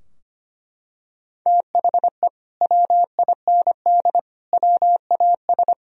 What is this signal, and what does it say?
Morse code, heard as a keyed tone, about 25 wpm, 700 Hz. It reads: THE WIND WAS
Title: Sentences from Top 1000 Words - Review Part 2 - 25wpm